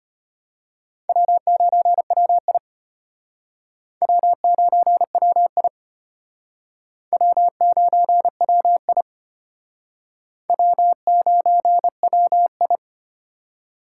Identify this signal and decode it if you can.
Morse code, heard as a keyed tone, about 32 wpm, 700 Hz. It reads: W9WI W9WI W9WI W9WI